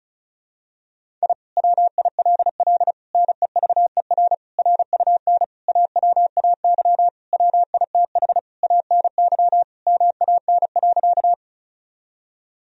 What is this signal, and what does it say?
Morse code, heard as a keyed tone, about 34 wpm, 700 Hz. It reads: I WILL NEVER RUN AWAY WITH ANY MAN.
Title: Sentences from Top 1000 Words - Review Part 1 - 35wpm